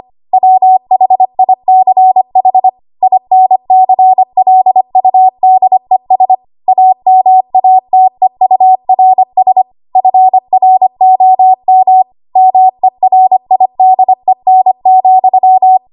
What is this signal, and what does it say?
Morse code, heard as a keyed tone, about 25 wpm, 750 Hz. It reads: WHICH INCLUDES AMATEURS FROM MERIDEN,
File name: Morse / 25WPM